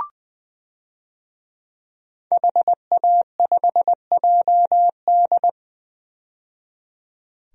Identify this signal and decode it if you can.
Morse code, heard as a keyed tone, about 20 wpm, 700 Hz. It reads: HA5JD
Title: DXCC-100 Call Signs - 5 Characters - 20wpm